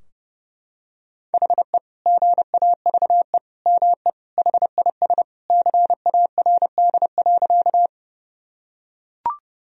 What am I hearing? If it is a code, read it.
Morse code, heard as a keyed tone, about 30 wpm, 700 Hz. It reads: HE GAVE ME HIS CARD.
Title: Sentences from Top 1000 Words - Review Part 4 - 30wpm